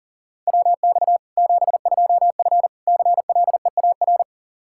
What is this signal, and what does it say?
Morse code, heard as a keyed tone, about 40 wpm, 700 Hz. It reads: WX 72F CLEAR
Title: Practice QSO - Exchange 2 - Info - 40wpm